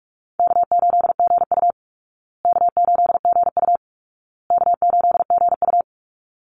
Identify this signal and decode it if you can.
Morse code, heard as a keyed tone, about 44 wpm, 700 Hz. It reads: K8GU K8GU K8GU
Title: Easy Call Signs Repeated 3x - 45wpm